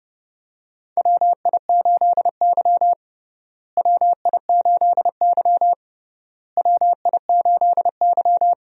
Morse code, heard as a keyed tone, about 30 wpm, 700 Hz. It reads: WI8Y WI8Y WI8Y
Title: USA Call Signs - 4 Characters - Repeated 3x - 30wpm